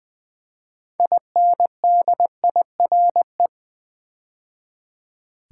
Morse code, heard as a keyed tone, about 20 wpm, 700 Hz. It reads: INDIRE
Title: Build A Word - 2 Letters At A Time - 8-Letter Words - 20wpm